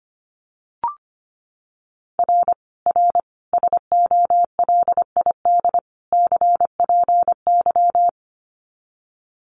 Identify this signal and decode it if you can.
Morse code, heard as a keyed tone, about 25 wpm, 700 Hz. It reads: R R SOLID CPY
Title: Common QSO Phrases 25wpm